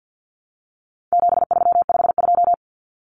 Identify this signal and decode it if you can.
Morse code, heard as a keyed tone, about 50 wpm, 700 Hz. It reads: 7352
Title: Numbers - 4-Digit 50wpm